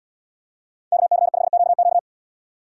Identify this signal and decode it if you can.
Morse code, heard as a keyed tone, about 77 wpm, 700 Hz. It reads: KC5QQ